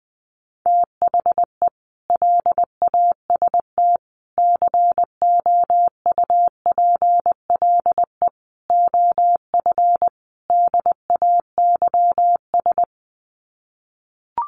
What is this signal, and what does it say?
Morse code, heard as a keyed tone, about 20 wpm, 700 Hz. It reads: THE LAST COUPLE OF DAYS
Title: Sets of 5 Words Encore 20wpm